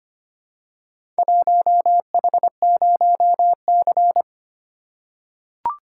Morse code, heard as a keyed tone, about 25 wpm, 700 Hz. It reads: 1H0C
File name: practice / 25wpm / Four Letter-Number Combinations - 25wpm